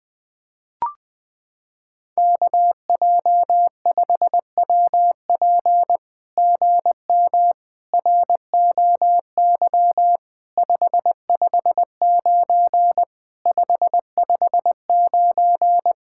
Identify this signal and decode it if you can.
Morse code, heard as a keyed tone, about 20 wpm, 700 Hz. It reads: KJ5WP GM ROY 559 559
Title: SOTA QSO Practice 20wpm